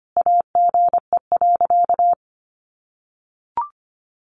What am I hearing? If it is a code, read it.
Morse code, heard as a keyed tone, about 25 wpm, 700 Hz. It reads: AGE.
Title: Sentences from Top 1400 Words 25wpm